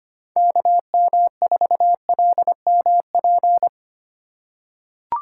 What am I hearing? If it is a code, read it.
Morse code, heard as a keyed tone, about 25 wpm, 700 Hz. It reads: KM4LMP
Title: USA Call Signs - 6 Characters - 3x Character Spacing - 25wpm